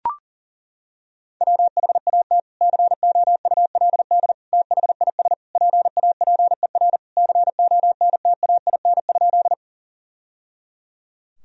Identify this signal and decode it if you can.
Morse code, heard as a keyed tone, about 40 wpm, 700 Hz. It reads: WHAT COULD THIS PAPER CONTAIN?
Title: Sentences from Top 1000 Words - Review Part 2 - 40wpm